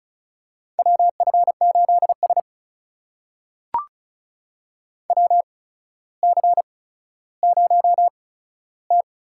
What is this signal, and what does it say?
Morse code, heard as a keyed tone, about 36 wpm, spaced as for 15 wpm, 700 Hz. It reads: WF8S WC0T
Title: USA Call Signs - 4 Characters - 8x Character Spacing - 35wpm